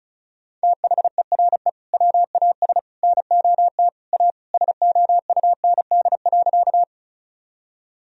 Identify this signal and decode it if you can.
Morse code, heard as a keyed tone, about 36 wpm, 700 Hz. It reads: THERE WAS NOT A SOUND.
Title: Sentences from Top 1000 Words - Review Part 1 - 35wpm